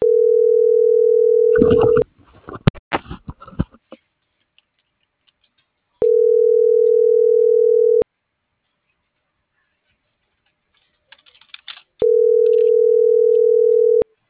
Ringback tone (North America), 440 + 480 Hz.
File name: CENTRAL SOTOVA / out alejandro.rodriguez - .51618